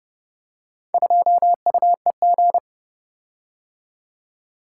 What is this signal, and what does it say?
Morse code, heard as a keyed tone, about 29 wpm, 700 Hz. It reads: 2UEG